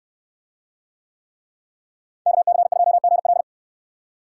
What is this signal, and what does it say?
Morse code, heard as a keyed tone, about 69 wpm, 700 Hz. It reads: KC2GL